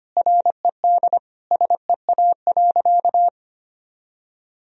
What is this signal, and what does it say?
Morse code, heard as a keyed tone, about 25 wpm, 700 Hz. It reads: RED SEA.
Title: Sentences from Top 1000 Words - Review Part 2 - 25wpm